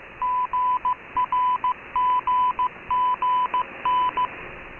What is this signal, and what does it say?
Morse code, heard as a keyed tone, about 15 wpm, 1000 Hz. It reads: GRGGN